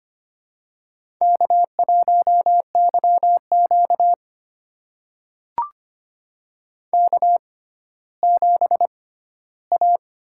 Morse code, heard as a keyed tone, about 25 wpm, spaced as for 14 wpm, 700 Hz. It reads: K1YQ K7A
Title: USA Call Signs - 4 Characters - 6x Character Spacing - 25wpm